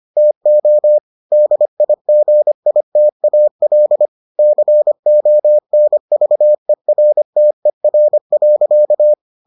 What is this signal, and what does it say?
Morse code, heard as a keyed tone, about 25 wpm, 600 Hz. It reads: TO DIGITAL CONVERTER.